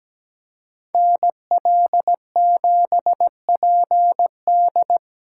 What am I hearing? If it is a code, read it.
Morse code, heard as a keyed tone, about 20 wpm, 700 Hz. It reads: NL7PD
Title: Speed Racing - USA Call Signs 17wpm